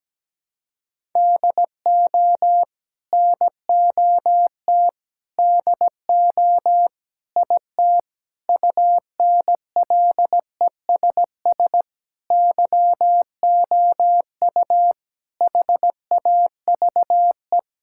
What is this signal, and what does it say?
Morse code, heard as a keyed tone, about 17 wpm, 700 Hz. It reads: DO NOT DO IT UNLESS YOU HAVE
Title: Sentences from Top 1000 Words - Review Part 6 - 17wpm